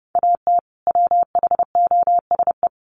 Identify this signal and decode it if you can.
Morse code, heard as a keyed tone, about 30 wpm, 700 Hz. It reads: AT WHOSE
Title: Sentences from Top 1200 Words 30wpm